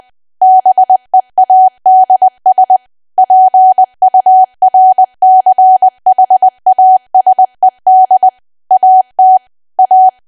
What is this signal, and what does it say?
Morse code, heard as a keyed tone, about 20 wpm, 750 Hz. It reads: BEADS PURCHASED AT A
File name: cwa / W1AW-Level3 / 20WPM